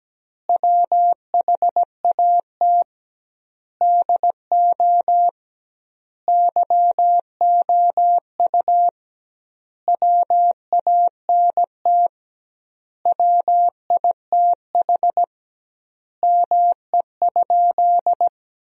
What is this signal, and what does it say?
Morse code, heard as a keyed tone, about 17 wpm, 700 Hz. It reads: WHAT DO YOU WANT WITH ME?
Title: 2x Word-Spacing - Sentences from Top 200 Words - 17wpm